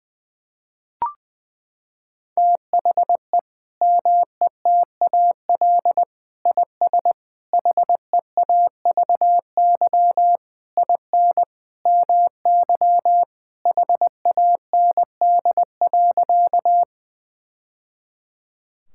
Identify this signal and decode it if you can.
Morse code, heard as a keyed tone, about 20 wpm, 700 Hz. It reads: THE METAL IS HEAVY IN MY HAND.
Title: Sentences from Top 1000 Words - Review Part 6 - 20wpm